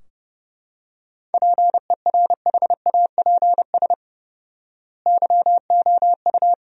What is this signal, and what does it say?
Morse code, heard as a keyed tone, about 32 wpm, 700 Hz. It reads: PERHAPS YOU
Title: 4x Word-Spacing - Sentences from Top 500 Words - 30wpm